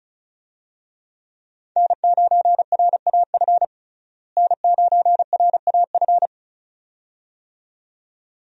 Morse code, heard as a keyed tone, about 36 wpm, 700 Hz. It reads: N9RAF N9RAF
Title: USA Call Signs - 5 Characters - Repeated 2x - 35wpm